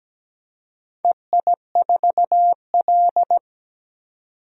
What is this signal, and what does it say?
Morse code, heard as a keyed tone, about 17 wpm, 700 Hz. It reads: EI4L
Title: DXCC-100 Call Signs - 4 Characters - 17wpm